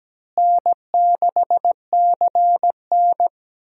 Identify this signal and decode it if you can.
Morse code, heard as a keyed tone, about 17 wpm, 700 Hz. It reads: N6CN
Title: USA Call Signs - 4 Characters - Repeated 3x - 17wpm